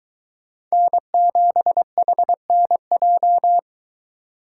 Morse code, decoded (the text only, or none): N7HNJ